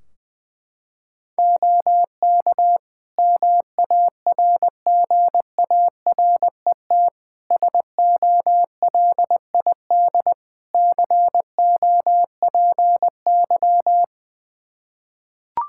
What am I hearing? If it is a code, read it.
Morse code, heard as a keyed tone, about 20 wpm, 700 Hz. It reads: OK MARGARET SOLID COPY
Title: Practice QSO - Exchange 3 - More Info - 20wpm